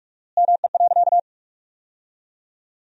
Morse code, heard as a keyed tone, about 46 wpm, 700 Hz. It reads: ME.
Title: 4x Word-Spacing - Sentences from Top 200 Words - 45wpm